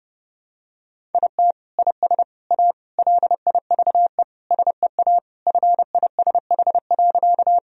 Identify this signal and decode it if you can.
Morse code, heard as a keyed tone, about 30 wpm, 700 Hz. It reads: IT IS A LIVE SEA FISH.
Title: Sentences from Top 1000 Words - Review Part 1 - 30wpm